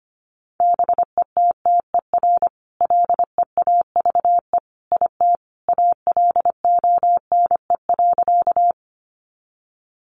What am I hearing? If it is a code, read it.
Morse code, heard as a keyed tone, about 25 wpm, 700 Hz. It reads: BETTER LEAVE IT ALONE.
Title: Sentences from Top 1000 Words 25wpm